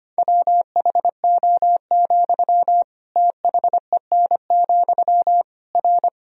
Morse code, heard as a keyed tone, about 25 wpm, 700 Hz. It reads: WHO, THEN, R
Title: Sentences from Top 100 Words and Abbreviations 25wpm